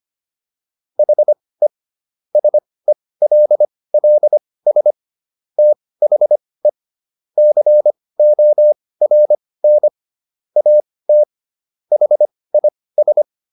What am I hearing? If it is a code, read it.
Morse code, heard as a keyed tone, about 25 wpm, spaced as for 18 wpm, 600 Hz. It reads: HE SELLS THE CORN AT HIS